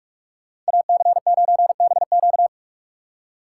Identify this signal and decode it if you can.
Morse code, heard as a keyed tone, about 59 wpm, spaced as for 51 wpm, 700 Hz. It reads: AC9BQ
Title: Speed Racing - USA Call Signs 45wpm